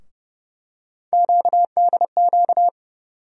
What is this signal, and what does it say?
Morse code, heard as a keyed tone, about 31 wpm, 700 Hz. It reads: QDQ
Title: USA Call Sign Postfixes - 3 Letters 30wpm